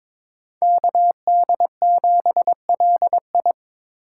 Morse code, heard as a keyed tone, about 26 wpm, 700 Hz. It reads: KD7LI
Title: Speed Racing - USA 2x2 Call Signs 22wpm